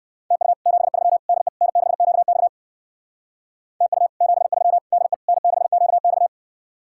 Morse code, heard as a keyed tone, about 58 wpm, 700 Hz. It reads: TU 73 DE N6QX TU 73 DE N6QX